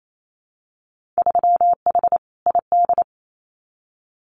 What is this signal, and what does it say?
Morse code, heard as a keyed tone, about 28 wpm, 700 Hz. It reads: 3H ID
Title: Winter Field Day Exchanges Encore 28wpm